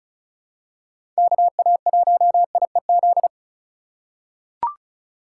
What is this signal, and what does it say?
Morse code, heard as a keyed tone, about 36 wpm, 700 Hz. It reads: KA1IEZ
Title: USA Call Signs - 6 Characters - 2x Character Spacing - 35wpm